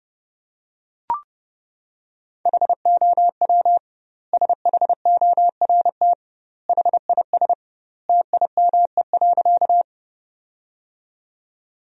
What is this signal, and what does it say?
Morse code, heard as a keyed tone, about 32 wpm, 700 Hz. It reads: HOW SHORT HIS TIME.
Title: 2x Word-Spacing - Sentences from Top 400 Words - 30wpm